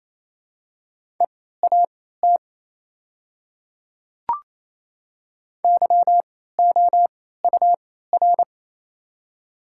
Morse code, heard as a keyed tone, about 28 wpm, spaced as for 16 wpm, 690 Hz. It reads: EAT YOUR
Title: 3x Character Spacing - Top 500 Words - 28wpm